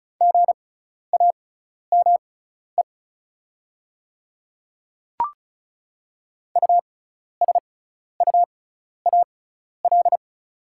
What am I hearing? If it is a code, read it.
Morse code, heard as a keyed tone, about 36 wpm, spaced as for 12 wpm, 690 Hz. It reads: GAME USUAL
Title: 6x Character Spacing - Top 500 Words - 35wpm